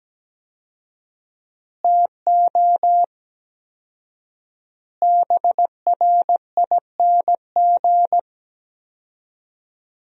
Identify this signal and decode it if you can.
Morse code, heard as a keyed tone, about 17 wpm, 700 Hz. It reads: TO BRING
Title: 4x Word-Spacing - Sentences from Top 500 Words - 17wpm